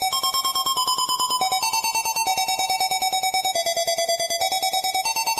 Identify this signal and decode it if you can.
There is no code.